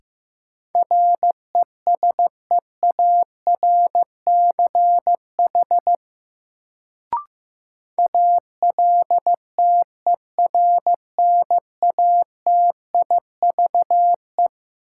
Morse code, heard as a keyed tone, about 15 wpm, 700 Hz. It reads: RESEARCH ALTERNATIVE
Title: Sentences from Top 1000 Words - Review Part 4 - New Word Drill - 15wpm